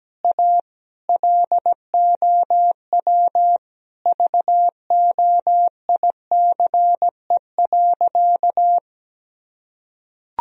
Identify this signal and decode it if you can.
Morse code, heard as a keyed tone, about 17 wpm, 700 Hz. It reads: A LOW VOICE.
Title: Sentences from Top 1000 Words - Review Part 1 - 17wpm